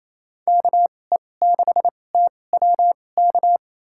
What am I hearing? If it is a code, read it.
Morse code, heard as a keyed tone, about 27 wpm, spaced as for 20 wpm, 700 Hz. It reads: KE6TWK
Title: USA Call Signs - 6 Characters - 2x Character Spacing - 28wpm